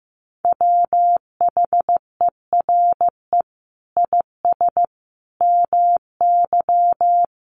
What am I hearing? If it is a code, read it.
Morse code, heard as a keyed tone, about 15 wpm, 700 Hz. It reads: WHERE IS MY
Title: Sentences from Top 1400 Words 15wpm